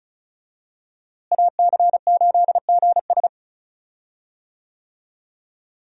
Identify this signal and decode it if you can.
Morse code, heard as a keyed tone, about 36 wpm, 700 Hz. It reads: AC8GS
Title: USA Call Signs - 5 Characters - Repeated 2x - 35wpm